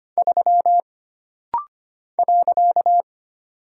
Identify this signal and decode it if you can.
Morse code, heard as a keyed tone, about 25 wpm, 700 Hz. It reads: 3 .
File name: Single Character (Letter:Number:Punctuation:Prosign) - Rapid-Fire - 25wpm